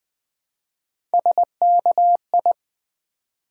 Morse code, heard as a keyed tone, about 20 wpm, 700 Hz. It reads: SKI
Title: Build A Word - 1 Letter At A Time - 4-Letter Words - 20wpm